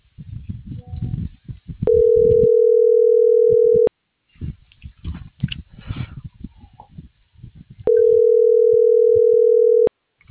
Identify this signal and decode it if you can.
Ringback tone (North America), 440 + 480 Hz.